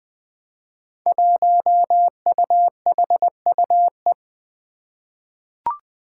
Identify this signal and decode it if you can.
Morse code, heard as a keyed tone, about 20 wpm, 700 Hz. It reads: 1UHUE
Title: Five Challenging Letter-Number Combinations 20wpm